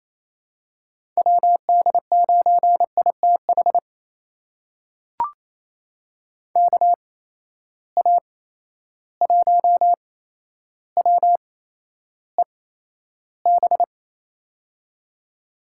Morse code, heard as a keyed tone, about 27 wpm, spaced as for 12 wpm, 700 Hz. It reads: WD9ITH KA1WEB
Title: USA Call Signs - 6 Characters - 8x Character Spacing - 28wpm